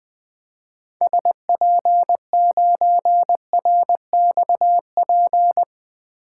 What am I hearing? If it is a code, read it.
Morse code, heard as a keyed tone, about 20 wpm, 700 Hz. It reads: SP9RXP
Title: 2023 POTA Hunter Call Signs - 20wpm